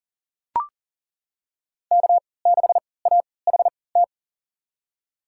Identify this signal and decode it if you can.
Morse code, heard as a keyed tone, about 40 wpm, spaced as for 23 wpm, 700 Hz. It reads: K6AHT
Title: USA Call Signs - 5 Characters - 3x Character Spacing - 40wpm